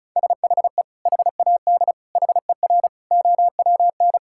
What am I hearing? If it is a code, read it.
Morse code, heard as a keyed tone, about 36 wpm, 700 Hz. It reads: SHE HAD HER OWN
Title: Sentences from Top 1000 Words - Review Part 5 - 35wpm